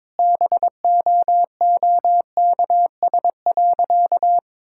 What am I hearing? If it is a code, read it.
Morse code, heard as a keyed tone, about 22 wpm, 700 Hz. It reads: BOOKS.
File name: practice / 22wpm / Sentences with 2-Word Phrases - 22wpm